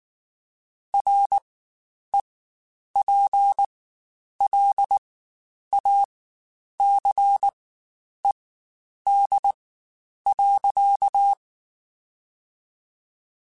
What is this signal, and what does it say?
Morse code, heard as a keyed tone, about 19 wpm, spaced as for 9 wpm, 800 Hz. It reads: REPLACED.
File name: Spaced / Chuckles Spaced-mp3s / Squeaky sp @18